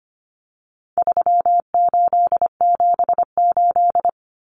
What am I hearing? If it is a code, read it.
Morse code, heard as a keyed tone, about 25 wpm, 700 Hz. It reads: 3878
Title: Numbers - 4-Digit 25wpm